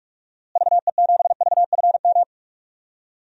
Morse code, heard as a keyed tone, about 47 wpm, 700 Hz. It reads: VE7VFM
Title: DXCC-100 Call Signs - 6 Characters - Repeated 3x - 45wpm